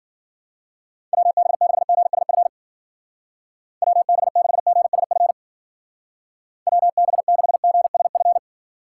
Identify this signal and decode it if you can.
Morse code, heard as a keyed tone, about 50 wpm, 700 Hz. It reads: WB6GSF WB6GSF WB6GSF